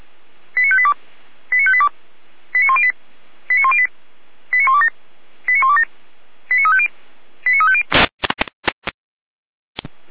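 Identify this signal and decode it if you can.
Five-tone selective call (ZVEI), heard as quick runs of five tones, tones of about 70 ms. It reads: 89582 89582 89108 89108 89127 89127 89350 89350
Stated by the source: ZVEI five-tone call, code 89409, received by SDR.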